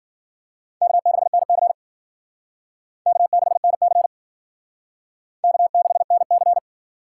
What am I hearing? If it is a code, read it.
Morse code, heard as a keyed tone, about 50 wpm, 700 Hz. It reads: K6NC K6NC K6NC
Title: Speed Racing - USA 1x2 Call Signs 40wpm